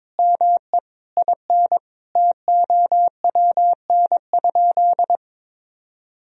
Morse code, heard as a keyed tone, about 22 wpm, 700 Hz. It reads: ME IN TOWN?
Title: Sentences from Top 500 Words and Abbreviations 22wpm